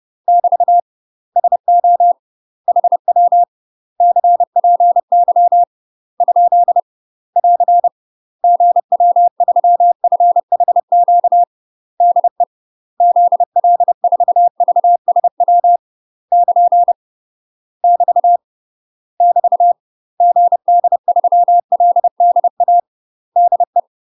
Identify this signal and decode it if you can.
Morse code, heard as a keyed tone, about 30 wpm, 700 Hz. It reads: <BT> SO HW CPY ? <AR> GW3FHQ DE ZL4VSW <KN> <BT> <BT> GD3LDA DE
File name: UK MASTER QSOs TRAINING / 30WPM / UK MASTER QSOs 0019